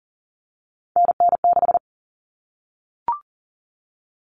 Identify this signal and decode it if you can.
Morse code, heard as a keyed tone, about 40 wpm, 700 Hz. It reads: NN6
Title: All Cut Numbers - 3-Digit 40wpm